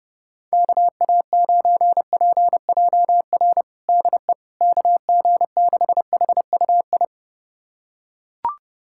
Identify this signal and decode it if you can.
Morse code, heard as a keyed tone, about 32 wpm, 700 Hz. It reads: KA9PJR DE KG6HUI
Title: USA Callsign DE Callsign 30wpm